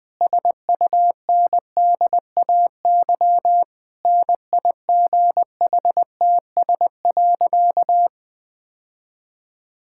Morse code, heard as a keyed tone, about 19 wpm, 700 Hz. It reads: SUNDAY NIGHTS.
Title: Difficult Sentences with Hints - 20wpm